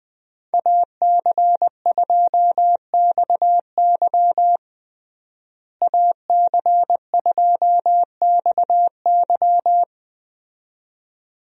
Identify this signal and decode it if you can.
Morse code, heard as a keyed tone, about 20 wpm, 700 Hz. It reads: AC2XY AC2XY
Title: USA Call Signs - 5 Characters - Repeated 2x - 20wpm